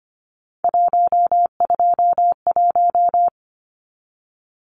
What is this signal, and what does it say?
Morse code, heard as a keyed tone, about 25 wpm, 700 Hz. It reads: 121
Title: Numbers - 3-Digit 25wpm